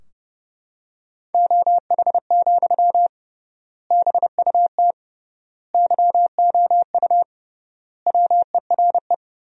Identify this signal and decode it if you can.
Morse code, heard as a keyed tone, about 30 wpm, 700 Hz. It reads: OH, BUT YOU WERE
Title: 3x Word-Spacing - Sentences from Top 500 Words - 30wpm